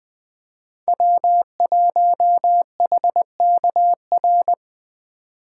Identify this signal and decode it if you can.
Morse code, heard as a keyed tone, about 20 wpm, 700 Hz. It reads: W1HKR
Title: Speed Racing - USA 1x3 Call Signs 17wpm